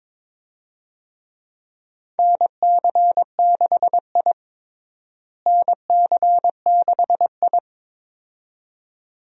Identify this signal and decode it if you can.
Morse code, heard as a keyed tone, about 22 wpm, 700 Hz. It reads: NC6I NC6I
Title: USA Call Signs - 4 Characters - Repeated 2x - 22wpm